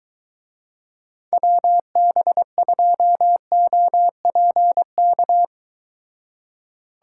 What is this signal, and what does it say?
Morse code, heard as a keyed tone, about 22 wpm, 700 Hz. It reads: WB2OPK